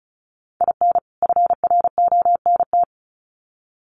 Morse code, heard as a keyed tone, about 34 wpm, 700 Hz. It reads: IN FRONT
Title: Sets of 2 Words 35wpm